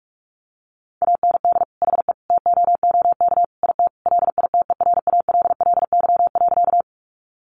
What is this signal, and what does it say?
Morse code, heard as a keyed tone, about 44 wpm, 700 Hz. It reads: AND HE TOOK IT LITERALLY.
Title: Sentences from Top 1400 Words 45wpm